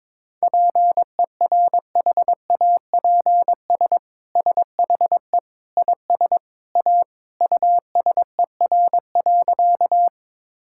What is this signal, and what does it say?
Morse code, heard as a keyed tone, about 22 wpm, 700 Hz. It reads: PERHAPS SHE IS A USER.
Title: Sentences from Top 1000 Words - Review Part 5 - 22wpm